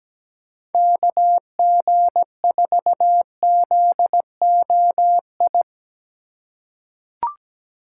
Morse code, heard as a keyed tone, about 17 wpm, 700 Hz. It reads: KG4ZOI
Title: USA Call Signs - 6 Characters - 3x Character Spacing - 17wpm